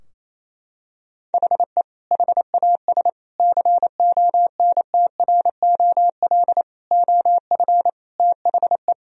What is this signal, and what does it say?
Morse code, heard as a keyed tone, about 27 wpm, 700 Hz. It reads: HE HAS CONTROL OF THE
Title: Sentences from Top 1000 Words - Review Part 3 - 28wpm